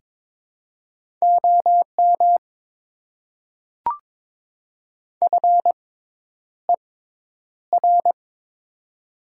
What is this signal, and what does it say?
Morse code, heard as a keyed tone, about 22 wpm, spaced as for 10 wpm, 700 Hz. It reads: OM FER
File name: Top 90 QSO Elements - 6x Character Spacing - 22wpm